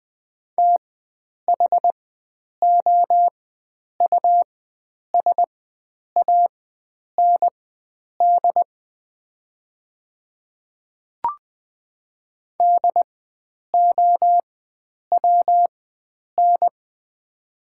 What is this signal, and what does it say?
Morse code, heard as a keyed tone, about 20 wpm, spaced as for 9 wpm, 700 Hz. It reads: THOUSAND DOWN
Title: 4x Character Spacing - Top 500 Words - 20wpm